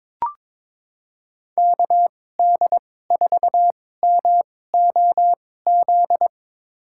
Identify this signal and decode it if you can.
Morse code, heard as a keyed tone, about 22 wpm, spaced as for 16 wpm, 700 Hz. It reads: KD4MOZ
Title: USA Call Signs - 6 Characters - 2x Character Spacing - 22wpm